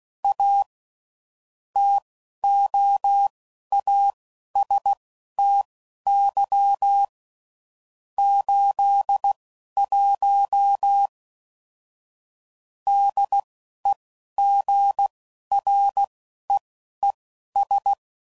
Morse code, heard as a keyed tone, about 16 wpm, spaced as for 11 wpm, 800 Hz. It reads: A TOASTY 81 DEGREES
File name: CW Code Groups @10WPM/Weather Sets